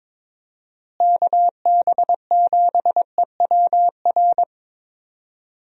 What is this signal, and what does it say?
Morse code, heard as a keyed tone, about 22 wpm, 700 Hz. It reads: KB7EWR
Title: USA Call Signs - 6 Characters - Repeated 2x - 22wpm